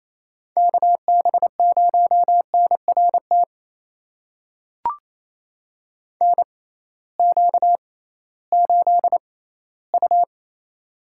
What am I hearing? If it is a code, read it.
Morse code, heard as a keyed tone, about 28 wpm, spaced as for 16 wpm, 700 Hz. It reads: KB0NRT NQ8U
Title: USA Call Signs - 6x Character Spacing - 28wpm